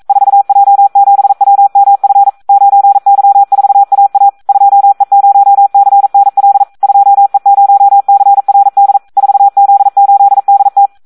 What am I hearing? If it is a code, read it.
Morse code, heard as a keyed tone, about 42 wpm, 800 Hz. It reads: 4J7WMF 9Y4AA 2E0CNL 2E0CRD 4Z8DT
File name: Audio practica CW / SD 1 42